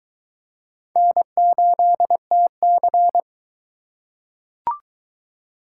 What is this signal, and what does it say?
Morse code, heard as a keyed tone, about 22 wpm, 700 Hz. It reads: N8TC